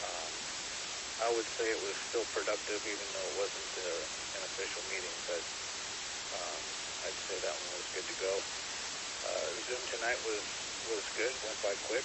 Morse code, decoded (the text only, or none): none